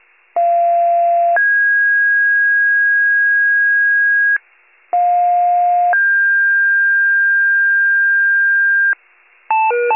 Two-tone page, heard as a one-second tone, then a three-second tone, about 690 then 1650 Hz, about 710 then 1600 Hz. It is heard from 0.4 to 8.9 s.
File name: pager / 41 RESCUE DUTY / wcalert